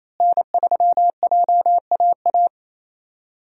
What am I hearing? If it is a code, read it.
Morse code, heard as a keyed tone, about 28 wpm, 700 Hz. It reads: N3JAA
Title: USA Call Signs - 5 Characters - Repeated 2x - 28wpm